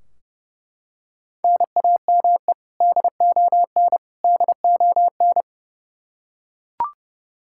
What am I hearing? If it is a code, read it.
Morse code, heard as a keyed tone, about 30 wpm, 700 Hz. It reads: NAME DON DON
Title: Practice QSO - Exchange 1 - Beginning - 30wpm